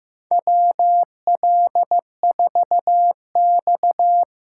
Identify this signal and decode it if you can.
Morse code, heard as a keyed tone, about 15 wpm, 700 Hz. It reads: WL4X
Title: USA Call Signs - 4 Characters - 6x Character Spacing - 15wpm